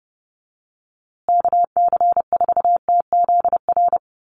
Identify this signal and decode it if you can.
Morse code, heard as a keyed tone, about 30 wpm, 700 Hz. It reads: KC4TZR
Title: USA Call Signs Repeated 3x 30wpm